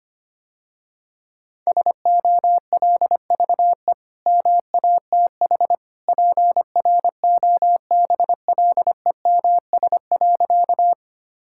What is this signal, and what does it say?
Morse code, heard as a keyed tone, about 25 wpm, 700 Hz. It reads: SOLVE MATH PROBLEMS.
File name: Easy Sentences with Hints - 25wpm